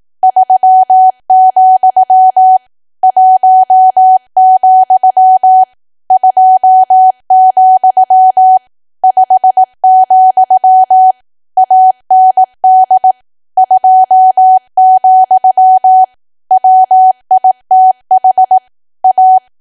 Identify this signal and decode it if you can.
Morse code, heard as a keyed tone, about 18 wpm, 750 Hz. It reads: 3, 1, 2, 5, AND 2, WITH A
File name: cwa / W1AW-Level2 / 18WPM